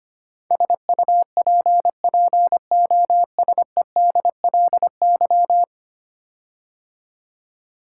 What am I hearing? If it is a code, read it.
Morse code, heard as a keyed tone, about 25 wpm, 700 Hz. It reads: SUPPOSEDLY
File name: 10 Letter Words - First 3 Letters Given - 25wpm